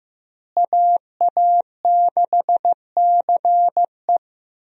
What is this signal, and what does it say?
Morse code, heard as a keyed tone, about 15 wpm, 700 Hz. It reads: AA6CE
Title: USA Call Signs - 5 Characters - Repeated 3x - 15wpm